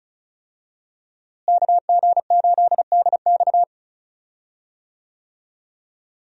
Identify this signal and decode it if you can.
Morse code, heard as a keyed tone, about 34 wpm, 700 Hz. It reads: KG8DX
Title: USA Call Signs - 5 Characters - Repeated 2x - 35wpm